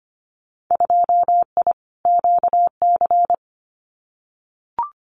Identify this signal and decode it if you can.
Morse code, heard as a keyed tone, about 25 wpm, 700 Hz. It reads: 2I QC
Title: Winter Field Day Exchanges 25wpm